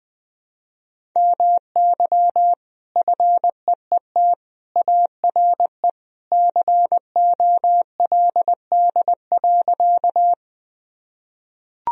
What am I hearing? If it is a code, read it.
Morse code, heard as a keyed tone, about 21 wpm, 700 Hz. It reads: MY FEET ARE COLD.
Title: Sentences from Top 1000 Words - Review Part 1 - 20wpm